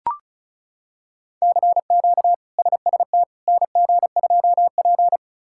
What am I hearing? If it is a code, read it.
Morse code, heard as a keyed tone, about 36 wpm, 700 Hz. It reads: CQ SST NG2P
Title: K1USN SST QSO Practice - 35wpm